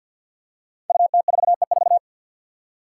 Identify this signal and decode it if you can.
Morse code, heard as a keyed tone, about 46 wpm, 700 Hz. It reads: UT4E4
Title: Five Challenging Letter-Number Combinations 50wpm